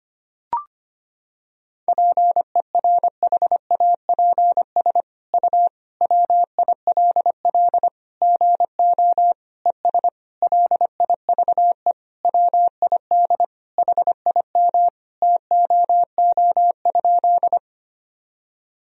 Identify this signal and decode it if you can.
Morse code, heard as a keyed tone, about 25 wpm, 700 Hz. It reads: PERHAPS U WILL GO ES LIVE WID HIM TOO?
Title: Sentences from Top 500 Words and Abbreviations 25wpm